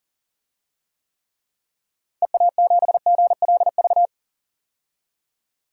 Morse code, heard as a keyed tone, about 40 wpm, 700 Hz. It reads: EA7GLV
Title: DXCC-100 Call Signs - 6 Characters - 40wpm